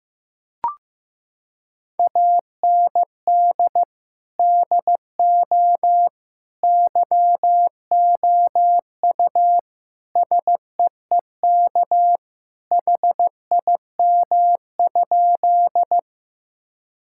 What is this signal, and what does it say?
Morse code, heard as a keyed tone, about 14 wpm, 700 Hz. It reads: AND DO YOU SEEK HIM?
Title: Sentences from Top 1600 Words 15wpm